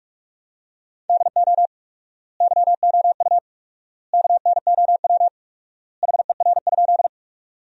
Morse code, heard as a keyed tone, about 41 wpm, 700 Hz. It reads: DO YOU KNOW HER?
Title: 4x Word-Spacing - Sentences from Top 100 Words - 45wpm